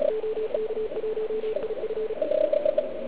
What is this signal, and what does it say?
Morse code, heard as a keyed tone, about 37 wpm, 430 Hz, with noise. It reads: OK1IF